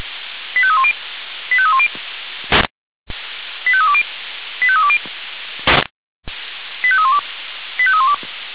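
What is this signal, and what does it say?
Five-tone selective call (ZVEI), heard as quick runs of five tones, tones of about 70 ms. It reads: 85310 85310 85320 85320 85312 85312
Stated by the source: ZVEI five-tone selective call 85310 received on SDR.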